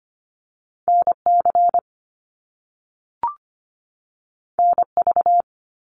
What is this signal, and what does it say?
Morse code, heard as a keyed tone, about 25 wpm, 700 Hz. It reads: NC NV
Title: US States 25wpm